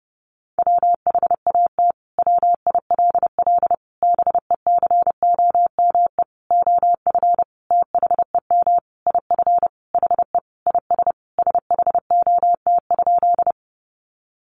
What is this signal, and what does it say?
Morse code, heard as a keyed tone, about 30 wpm, 700 Hz. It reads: WHAT WILL BECOME OF THEM IF HE IS SHOT?
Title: Sentences from Top 1100 Words 30wpm